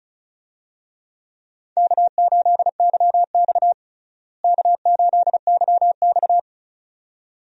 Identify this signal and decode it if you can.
Morse code, heard as a keyed tone, about 36 wpm, 700 Hz. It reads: K8YX K8YX
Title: USA Call Signs - 4 Characters - Repeated 2x - 35wpm